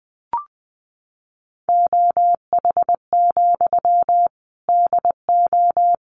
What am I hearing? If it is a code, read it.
Morse code, heard as a keyed tone, about 20 wpm, 700 Hz. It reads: OH, DO
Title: Sentences from Top 500 Words 20wpm